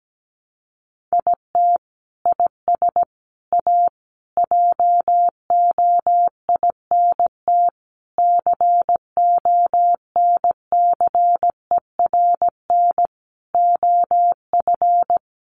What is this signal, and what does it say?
Morse code, heard as a keyed tone, about 17 wpm, 700 Hz. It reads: IT IS A JOINT CONCERN OF
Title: Sentences from Top 1300 Words 17wpm